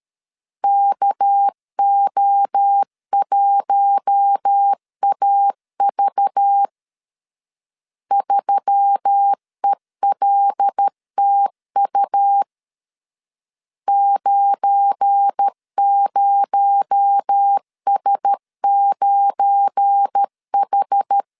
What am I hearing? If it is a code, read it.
Morse code, heard as a keyed tone, about 12 wpm, 800 Hz. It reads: KO1AV 3ELTU 90S9H
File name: LEZIONI / 7 GF9Z / 7 GF9 12wpm 2319